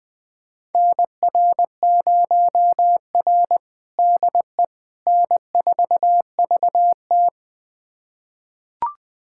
Morse code, heard as a keyed tone, about 20 wpm, 700 Hz. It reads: NR0R DE N4VT
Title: USA Callsign DE Callsign - 4 Characters - 20wpm